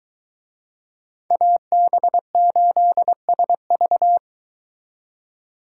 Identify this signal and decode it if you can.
Morse code, heard as a keyed tone, about 23 wpm, 700 Hz. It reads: AB8SV